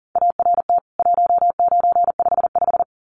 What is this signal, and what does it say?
Morse code, heard as a keyed tone, about 40 wpm, 700 Hz. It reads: ART 1955
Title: CWT Exchange 40wpm